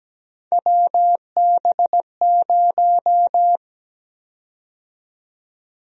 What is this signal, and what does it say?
Morse code, heard as a keyed tone, about 17 wpm, 700 Hz. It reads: WB0
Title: USA Call Sign Prefixes - 17wpm